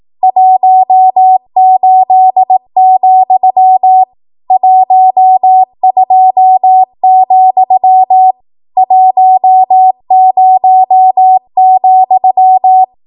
Morse code, heard as a keyed tone, about 18 wpm, 750 Hz. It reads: 18, 12, 10,
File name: Archive / 18 WPM / 18WPM